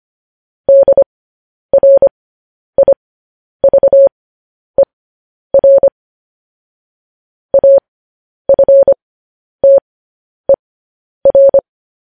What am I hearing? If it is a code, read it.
Morse code, heard as a keyed tone, about 25 wpm, spaced as for 10 wpm, 550 Hz. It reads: DRIVER AFTER
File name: podcast / News headlines-1x-25x10